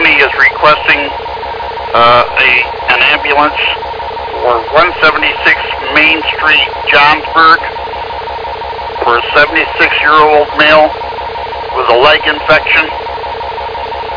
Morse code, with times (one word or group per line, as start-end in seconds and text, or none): none